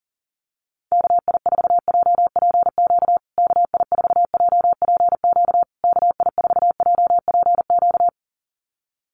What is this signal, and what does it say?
Morse code, heard as a keyed tone, about 40 wpm, 700 Hz. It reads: KI4JPQ KI4JPQ KI4JPQ
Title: USA Call Signs Repeated 3x 40wpm